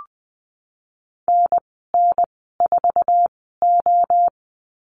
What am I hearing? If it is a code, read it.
Morse code, heard as a keyed tone, about 20 wpm, spaced as for 15 wpm, 700 Hz. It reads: NN4O
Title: 048 Lesson 9g - Number 4 - Call Signs 20wpm